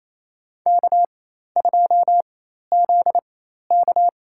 Morse code, heard as a keyed tone, about 28 wpm, spaced as for 13 wpm, 700 Hz. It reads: K2ZK
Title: USA Call Signs - 4x Character Spacing - 28wpm